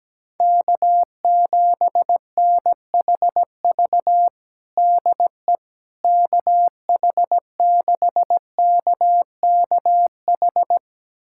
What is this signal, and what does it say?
Morse code, heard as a keyed tone, about 17 wpm, 700 Hz. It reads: K7NHV DE KH6KKH
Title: USA Callsign DE Callsign 17wpm